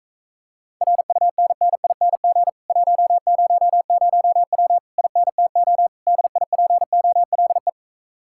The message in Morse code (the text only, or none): RUNNING 100W INTO DIPOLE